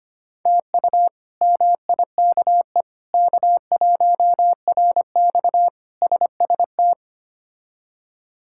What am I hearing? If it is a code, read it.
Morse code, heard as a keyed tone, about 25 wpm, 700 Hz. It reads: TU MIKE K1RX SST
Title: K1USN SST QSO Practice - 25wpm